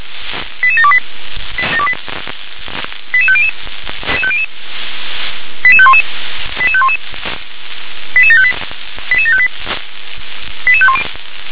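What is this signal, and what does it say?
Five-tone selective call (ZVEI), heard as quick runs of five tones, tones of about 70 ms. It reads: 88728 88500 88511 88511 88758 88758 88510
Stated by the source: ZVEI five-tone selective call 88730, received by SDR.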